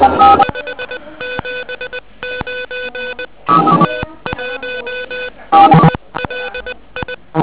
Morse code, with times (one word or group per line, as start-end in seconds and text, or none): 0.4-3.3 579
3.9-5.3 T1
5.9-7.1 ELI